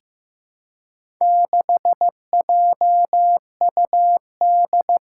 Morse code, heard as a keyed tone, about 15 wpm, 700 Hz. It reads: 6JUD
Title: Four Challenging Letter-Number Combinations 15wpm